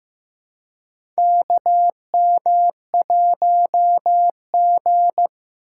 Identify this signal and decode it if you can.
Morse code, heard as a keyed tone, about 15 wpm, 700 Hz. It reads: KM1G
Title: USA Call Signs - 4 Characters - 4x Character Spacing - 15wpm